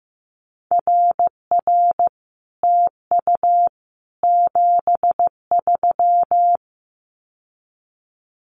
Morse code, heard as a keyed tone, about 15 wpm, 700 Hz. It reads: RR TU 73
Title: Common QSO Phrases Encore 15wpm